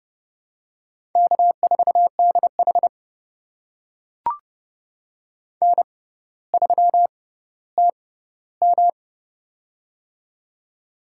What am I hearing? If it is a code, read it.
Morse code, heard as a keyed tone, about 30 wpm, spaced as for 15 wpm, 700 Hz. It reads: K4DH N3TM
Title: USA Call Signs - 4 Characters - 6x Character Spacing - 30wpm